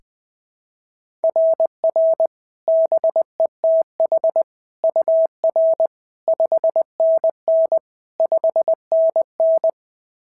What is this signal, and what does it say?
Morse code, heard as a keyed tone, about 20 wpm, 690 Hz. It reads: RR BETH UR 5NN 5NN
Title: SOTA QSO Practice 20wpm